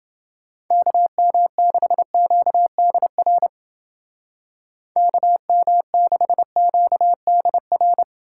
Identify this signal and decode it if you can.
Morse code, heard as a keyed tone, about 28 wpm, 700 Hz. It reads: KM6QDR KM6QDR